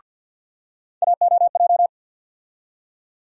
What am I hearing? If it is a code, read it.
Morse code, heard as a keyed tone, about 53 wpm, 700 Hz. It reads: AOJ
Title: USA Call Sign Postfixes - 3 Letters 50wpm